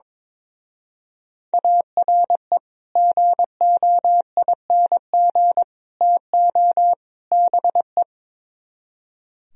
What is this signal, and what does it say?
Morse code, heard as a keyed tone, about 22 wpm, 700 Hz. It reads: ARE GOING TO BE
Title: Speed-Racing - Sets of 4 Words 22wpm